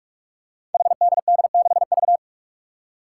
Morse code, heard as a keyed tone, about 44 wpm, 700 Hz. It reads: HDD6V
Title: Five Challenging Letter-Number Combinations 45wpm